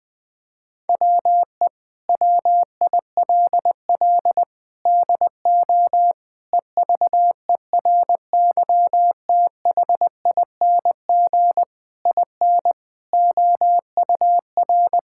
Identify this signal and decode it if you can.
Morse code, heard as a keyed tone, about 21 wpm, 700 Hz. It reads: WE WILL DO EVERYTHING IN OUR
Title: Sets of 8 Words - 20wpm